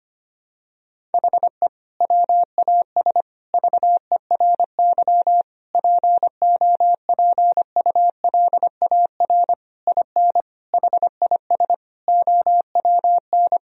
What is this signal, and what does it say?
Morse code, heard as a keyed tone, about 25 wpm, 700 Hz. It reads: HE WAS VERY POPULAR IN HIS OWN
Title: Sentences from Top 1000 Words - Review Part 3 - 25wpm